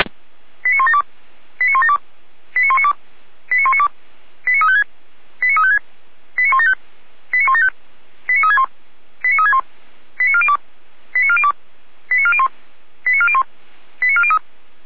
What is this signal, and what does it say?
Five-tone selective call (ZVEI), heard as quick runs of five tones, tones of about 70 ms. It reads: 89172 89172 89192 89192 89356 89356 89175 89175 89361 89361 89402 89402 89401 89401 89403